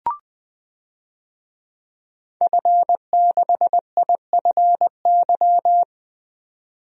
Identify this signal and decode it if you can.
Morse code, heard as a keyed tone, about 20 wpm, 700 Hz. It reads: F6IFY
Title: DXCC-100 Call Signs - 5 Characters - 20wpm